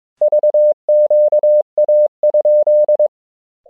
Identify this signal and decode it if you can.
Morse code, heard as a keyed tone, about 23 wpm, 600 Hz. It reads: VQA?
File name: ja6vqa-g8bcg